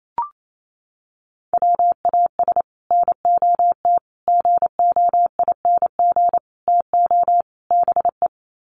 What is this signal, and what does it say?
Morse code, heard as a keyed tone, about 28 wpm, 700 Hz. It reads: WAS NOT GOING TO BE
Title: Sets of 5 Words Encore 28wpm